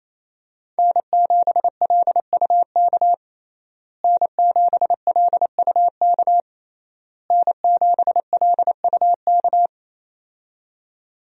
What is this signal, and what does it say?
Morse code, heard as a keyed tone, about 27 wpm, 700 Hz. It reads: N7LUK N7LUK N7LUK
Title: USA Call Signs - 5 Characters - Repeated 3x - 28wpm